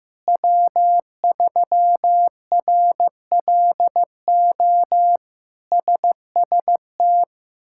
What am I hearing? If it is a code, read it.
Morse code, heard as a keyed tone, about 15 wpm, 700 Hz. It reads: W3RLO SST
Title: K1USN SST QSO Practice - 15wpm